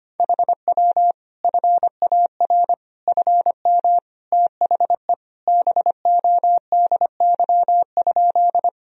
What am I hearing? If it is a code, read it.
Morse code, heard as a keyed tone, about 25 wpm, 700 Hz. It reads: HW FAR FM THE BODY?
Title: Sentences from Top 400 Words and Abbreviations 25wpm